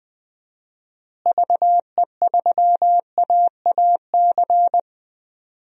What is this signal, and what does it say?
Morse code, heard as a keyed tone, about 20 wpm, 700 Hz. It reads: VE3AAC
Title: DXCC-100 Call Signs - 6 Characters - Repeated 3x - 20wpm